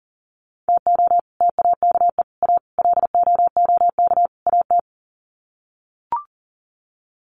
Morse code, heard as a keyed tone, about 40 wpm, 700 Hz. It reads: TO TAKE A LOOK AT
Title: Sets of 5 Words 40wpm